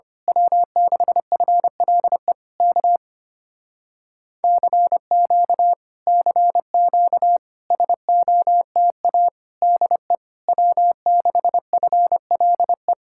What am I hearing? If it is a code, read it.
Morse code, heard as a keyed tone, about 25 wpm, 700 Hz. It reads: W6FLE K CQ CQ SOTA DE W6FLE
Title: Speed-Racing Common QSO Phrases - 25wpm